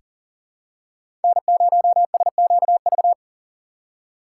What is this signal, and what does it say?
Morse code, heard as a keyed tone, about 37 wpm, 700 Hz. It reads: N0SQV
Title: USA Call Signs - 5 Characters - 4x Character Spacing - 40wpm